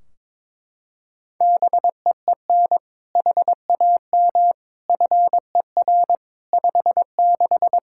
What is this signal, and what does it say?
Morse code, heard as a keyed tone, about 22 wpm, 700 Hz. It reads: BEEN HAM FER 56
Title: Practice QSO - Exchange 3 - More Info - 22wpm